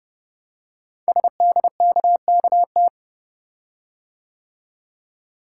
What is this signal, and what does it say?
Morse code, heard as a keyed tone, about 30 wpm, 700 Hz. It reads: SDKKT
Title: Code Groups - 5 Letters - 30wpm